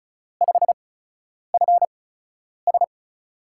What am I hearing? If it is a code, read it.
Morse code, heard as a keyed tone, about 36 wpm, spaced as for 10 wpm, 700 Hz. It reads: 5FS
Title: USA Call Signs - 5 Characters - 8x Character Spacing - 35wpm